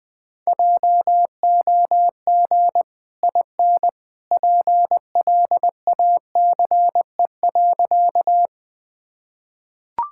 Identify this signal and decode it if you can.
Morse code, heard as a keyed tone, about 20 wpm, 700 Hz. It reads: JOG IN PLACE.
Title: Easy Sentences with Hints - 20wpm